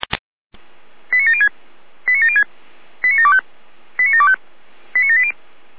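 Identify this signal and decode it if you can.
Five-tone selective call (ZVEI), heard as quick runs of five tones, tones of about 70 ms. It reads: 89706 89706 89725 89725 89780
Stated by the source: ZVEI five-tone call received by SDR.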